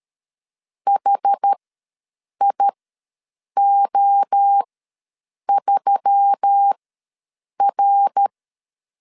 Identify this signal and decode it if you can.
Morse code, heard as a keyed tone, about 13 wpm, spaced as for 7 wpm, 800 Hz. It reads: HIO3R
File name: LEZIONI / 7 GF9Z / 7 GF9 ws 12wpm 3344